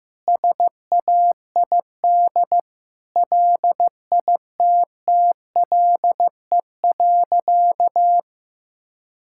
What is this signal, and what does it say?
Morse code, heard as a keyed tone, about 15 wpm, 700 Hz. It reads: SAID LITTLE.
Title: Sentences from Top 1600 Words 15wpm